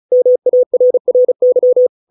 Morse code, heard as a keyed tone, about 34 wpm, 500 Hz. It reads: MARRY